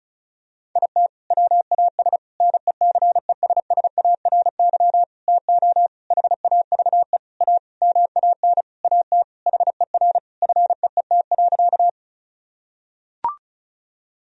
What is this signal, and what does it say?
Morse code, heard as a keyed tone, about 36 wpm, 700 Hz. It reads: IT WAS NECESSARY TO HAVE A MAN AT HER FEET.
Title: Sentences from Top 1000 Words - Review Part 1 - 35wpm